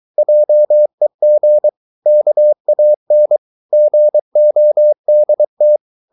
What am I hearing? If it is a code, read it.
Morse code, heard as a keyed tone, about 23 wpm, 600 Hz. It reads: JEG KAN GODT
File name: Ferie QRS 23wpm